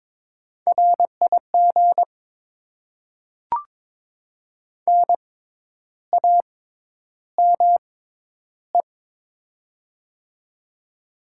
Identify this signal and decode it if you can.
Morse code, heard as a keyed tone, about 22 wpm, spaced as for 10 wpm, 700 Hz. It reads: RIG NAME
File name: Top 20 QSO Elements - 6x Character Spacing - 22wpm